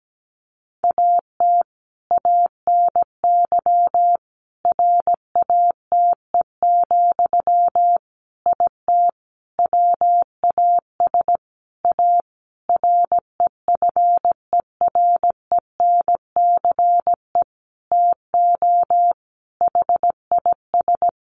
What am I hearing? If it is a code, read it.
Morse code, heard as a keyed tone, about 17 wpm, 700 Hz. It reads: AT ANY RATE, IT WAS A REFERENCE TO HIS
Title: Sentences from Top 1200 Words 17wpm